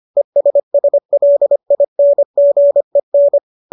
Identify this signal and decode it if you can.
Morse code, heard as a keyed tone, about 25 wpm, 570 Hz. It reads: ESSLINGEN